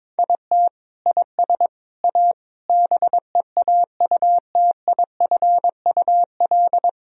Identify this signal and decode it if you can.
Morse code, heard as a keyed tone, about 22 wpm, 700 Hz. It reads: IT IS A BEAUTIFUL
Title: Sentences from Top 1000 Words - Review Part 2 - 22wpm